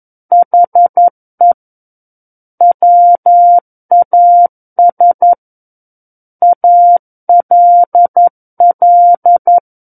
Morse code, heard as a keyed tone, about 11 wpm, 700 Hz. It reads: HE WAS ALL
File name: Book One / 02-Chapter 2 - 11 WPM